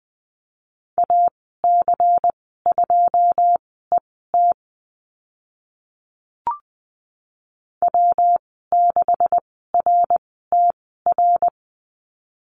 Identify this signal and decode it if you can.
Morse code, heard as a keyed tone, about 20 wpm, spaced as for 15 wpm, 700 Hz. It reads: AC2ET W6RTR
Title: 140 Lesson 23g - Letter f - Call Signs 20wpm